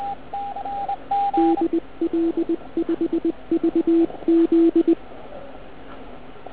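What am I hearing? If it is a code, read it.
Morse code, heard as a keyed tone, about 20 wpm, 330 Hz, with radio noise. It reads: DL5VZ
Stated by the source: Morning 80m pile-up recorded at nearly the same time on three transceivers: FT1000MKV, FT817 and ATS-3B.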